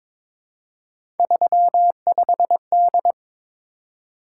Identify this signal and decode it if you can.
Morse code, heard as a keyed tone, about 23 wpm, 700 Hz. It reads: 35D
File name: practice / 22wpm / Three Letter-Number Combinations - 22wpm